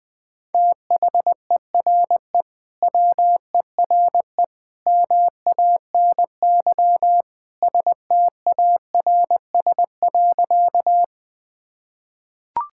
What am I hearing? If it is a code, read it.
Morse code, heard as a keyed tone, about 20 wpm, 700 Hz. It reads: THERE WERE MANY STARS.
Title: Sentences with 2-Word Phrases - 20wpm